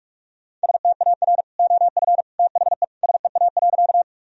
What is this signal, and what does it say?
Morse code, heard as a keyed tone, about 46 wpm, 700 Hz. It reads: STAR OF THE SEA.
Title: Sentences from Top 1000 Words - Review Part 2 - 45wpm